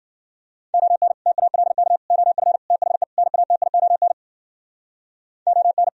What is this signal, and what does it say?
Morse code, heard as a keyed tone, about 49 wpm, 700 Hz. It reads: ON TALK OF THE NATION ON
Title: Speed-Racing - Sets of 5 Words 40wpm